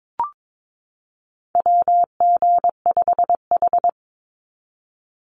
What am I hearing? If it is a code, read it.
Morse code, heard as a keyed tone, about 22 wpm, 700 Hz. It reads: WG5H
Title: Call Signs - 22wpm